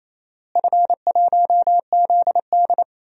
Morse code, heard as a keyed tone, about 28 wpm, 700 Hz. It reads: F1ZD
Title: DXCC-100 Call Signs - 4 Characters - Repeated 2x - 28wpm